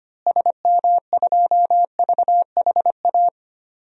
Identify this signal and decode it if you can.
Morse code, heard as a keyed tone, about 25 wpm, 700 Hz. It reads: SM2VHA
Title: DXCC-100 Call Signs - 6 Characters - Repeated 2x - 25wpm